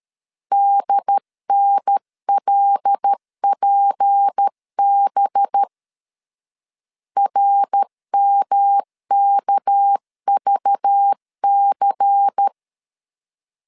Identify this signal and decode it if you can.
Morse code, heard as a keyed tone, about 13 wpm, 800 Hz. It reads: DNLPB RMKVC
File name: LEZIONI / 7 GF9Z / 7 G 12wpm 2055